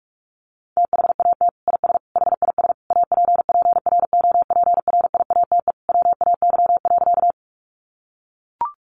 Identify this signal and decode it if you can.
Morse code, heard as a keyed tone, about 44 wpm, 700 Hz. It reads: THAT IS HIS APPROPRIATE WAY.
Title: Sentences from Top 900 Words 45wpm